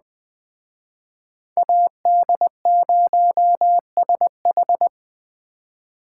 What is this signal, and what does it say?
Morse code, heard as a keyed tone, about 20 wpm, 700 Hz. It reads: AD0SH